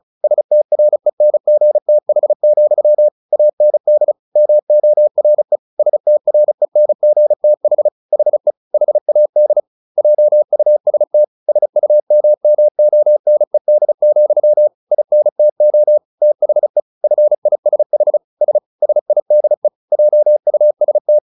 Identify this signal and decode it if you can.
Morse code, heard as a keyed tone, about 34 wpm, 600 Hz. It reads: STRENGTH, AND MORE STRENGTH HE HAD JUST SUMMONED, INTO THE FISH S SIDE JUST
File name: the old man and the sea 2 35wpm